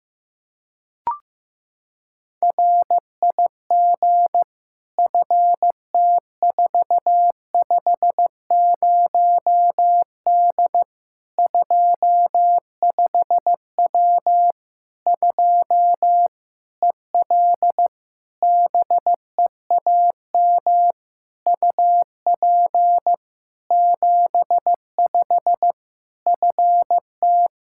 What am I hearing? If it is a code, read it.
Morse code, heard as a keyed tone, about 15 wpm, 700 Hz. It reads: RIG FT450D 25W 2 EL BEAM UP 75 FT
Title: Practice QSO - Exchange 2 - Info - 15wpm